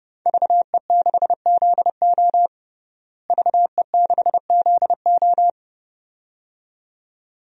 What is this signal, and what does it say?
Morse code, heard as a keyed tone, about 30 wpm, 700 Hz. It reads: VE6ZO VE6ZO
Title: DXCC-100 Call Signs - 5 Characters - Repeated 2x - 30wpm